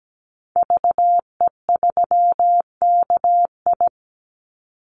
Morse code, heard as a keyed tone, about 17 wpm, 700 Hz. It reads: VE3KI
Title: Call Signs - 17wpm